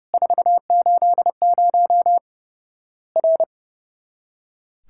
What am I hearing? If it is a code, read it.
Morse code, heard as a keyed tone, about 30 wpm, 700 Hz. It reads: 480 R
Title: Worked All Europe DX Contest - QTC Exchange - 30wpm